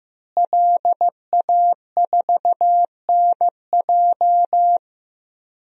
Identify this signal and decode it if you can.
Morse code, heard as a keyed tone, about 15 wpm, 700 Hz. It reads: LA4NJ
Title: DXCC-100 Call Signs - 5 Characters - Repeated 2x - 15wpm